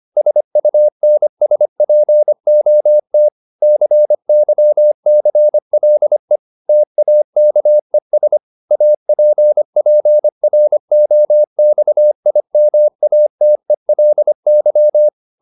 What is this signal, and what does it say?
Morse code, heard as a keyed tone, about 25 wpm, 600 Hz. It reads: SUNSPOT CYCLE TAKES APPROXIMATELY